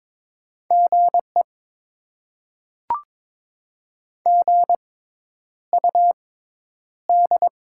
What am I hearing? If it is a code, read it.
Morse code, heard as a keyed tone, about 22 wpm, spaced as for 10 wpm, 690 Hz. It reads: GE GUD
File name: Top 20 QSO Elements - 6x Character Spacing - 22wpm